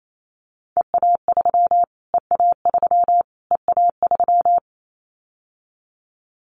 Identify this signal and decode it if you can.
Morse code, heard as a keyed tone, about 28 wpm, 700 Hz. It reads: EA3 EA3 EA3
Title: Callsign Prefixes for Top 25 DXCC - Repeated 3x - 28wpm